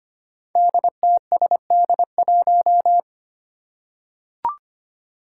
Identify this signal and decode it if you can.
Morse code, heard as a keyed tone, about 25 wpm, 700 Hz. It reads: DTSD1
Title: Five Challenging Letter-Number Combinations 25wpm